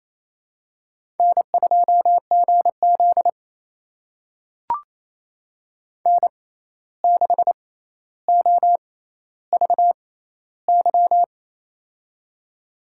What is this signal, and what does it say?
Morse code, heard as a keyed tone, about 28 wpm, spaced as for 13 wpm, 700 Hz. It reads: N2GZ N6OVY
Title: USA Call Signs - 6x Character Spacing - 28wpm